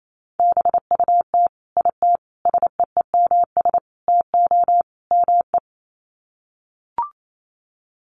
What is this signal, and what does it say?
Morse code, heard as a keyed tone, about 28 wpm, 700 Hz. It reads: BUT IT SEEMS TO ME
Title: Sets of 5 Words Encore2 28wpm